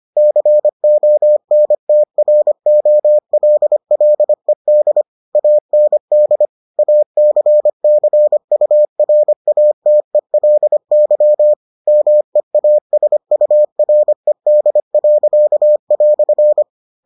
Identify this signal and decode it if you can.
Morse code, heard as a keyed tone, about 25 wpm, 600 Hz. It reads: CONTROLLED AND ACCURATELY MEASURED."